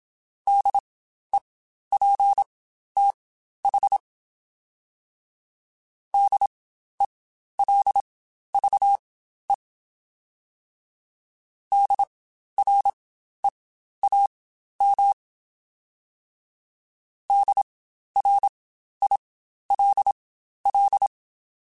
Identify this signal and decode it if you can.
Morse code, heard as a keyed tone, about 27 wpm, spaced as for 12 wpm, 800 Hz. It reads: DEPTH DELVE DREAM DRILL
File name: Spaced Alphas / Spaced Alphas mp3s / Ds sp @25